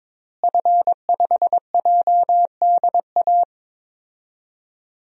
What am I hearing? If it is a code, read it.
Morse code, heard as a keyed tone, about 22 wpm, 700 Hz. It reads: F5JDA
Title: DXCC-100 Call Signs - 5 Characters - Repeated 3x - 22wpm